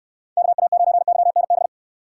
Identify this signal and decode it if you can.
Morse code, heard as a keyed tone, about 66 wpm, 700 Hz. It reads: KA9YNB